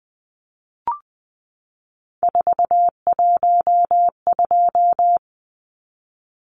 Morse code, heard as a keyed tone, about 20 wpm, 700 Hz. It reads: 412
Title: Numbers - 3-Digit 20wpm